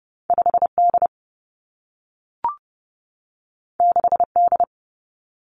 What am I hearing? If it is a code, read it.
Morse code, heard as a keyed tone, about 30 wpm, 700 Hz. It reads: 5 D 6 D
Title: Two Challenging Letter-Number Combinations 30wpm